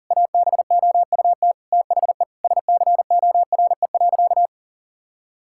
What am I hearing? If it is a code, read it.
Morse code, heard as a keyed tone, about 40 wpm, 700 Hz. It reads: ABOUT THE SCORE.
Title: Sentences from Top 1600 Words 40wpm